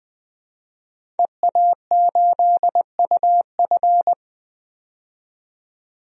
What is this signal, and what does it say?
Morse code, heard as a keyed tone, about 20 wpm, 700 Hz. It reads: EA8UF
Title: DXCC-100 Call Signs - 5 Characters - 20wpm